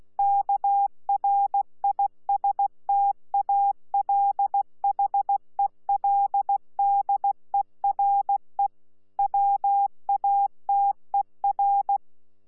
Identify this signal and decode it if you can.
Morse code, heard as a keyed tone, about 16 wpm, 800 Hz. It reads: KRISTALHELDERE WATER
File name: Z PI4VRZ / PI4VRZ 14-jun-2025